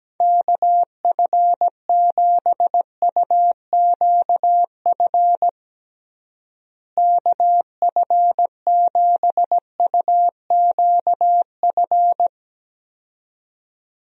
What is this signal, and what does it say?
Morse code, heard as a keyed tone, about 17 wpm, 700 Hz. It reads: KF7UQF KF7UQF
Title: USA Call Signs - 6 Characters - Repeated 2x - 17wpm